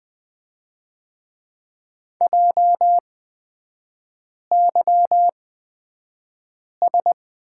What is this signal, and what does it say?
Morse code, heard as a keyed tone, about 20 wpm, spaced as for 5 wpm, 700 Hz. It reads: JYS
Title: 250 Lesson 40e - BK - Three Characters 20wpm